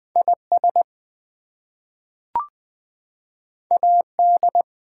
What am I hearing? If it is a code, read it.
Morse code, heard as a keyed tone, about 20 wpm, 700 Hz. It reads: IS AD
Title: Two Letter Words - 20wpm